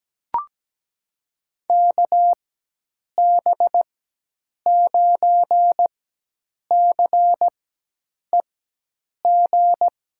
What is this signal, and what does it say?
Morse code, heard as a keyed tone, about 17 wpm, spaced as for 8 wpm, 700 Hz. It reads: KB9CEG
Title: USA Call Signs - 6 Characters - 4x Character Spacing - 17wpm